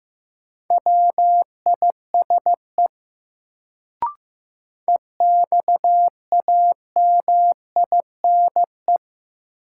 Morse code, heard as a keyed tone, about 15 wpm, 700 Hz. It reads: WISE EXAMINE
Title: Sentences from Top 1600 Words - New Word Drill Alt - 15wpm